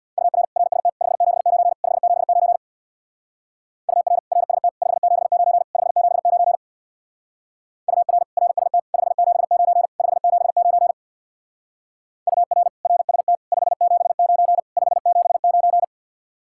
Morse code, heard as a keyed tone, about 60 wpm, 700 Hz. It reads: UR RST 579 579 UR RST 579 579 UR RST 579 579 UR RST 579 579
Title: Speed-Racing Common QSO Phrases - 50wpm